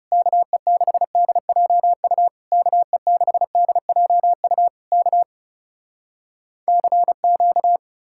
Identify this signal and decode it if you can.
Morse code, heard as a keyed tone, about 40 wpm, 700 Hz. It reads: KE6DJU KE6DJU K CQ